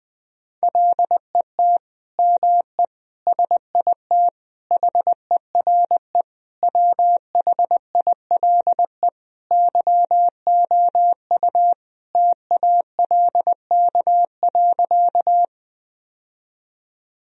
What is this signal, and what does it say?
Morse code, heard as a keyed tone, about 21 wpm, 700 Hz. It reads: LET ME SIT HERE WHILE YOU TALK.
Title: Sentences from Top 1000 Words - Review Part 1 - 20wpm